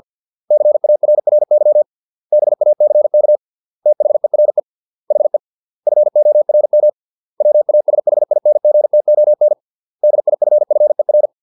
Morse code, heard as a keyed tone, about 50 wpm, 600 Hz. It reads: CARRY BACK THERE HE FORM WASHINGTON DIFFER